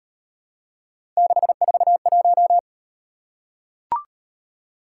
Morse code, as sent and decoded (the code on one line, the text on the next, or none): -.... ....- .----
641